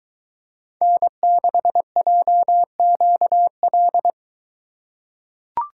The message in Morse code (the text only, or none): N6JQL